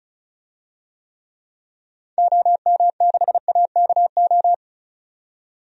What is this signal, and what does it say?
Morse code, heard as a keyed tone, about 34 wpm, 700 Hz. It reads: OM6AKO
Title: DXCC-100 Call Signs - 6 Characters - 35wpm